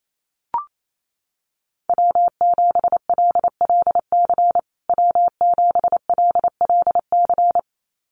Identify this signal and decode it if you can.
Morse code, heard as a keyed tone, about 28 wpm, 700 Hz. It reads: W7LLC W7LLC
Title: USA Call Signs Repeated 2x 28wpm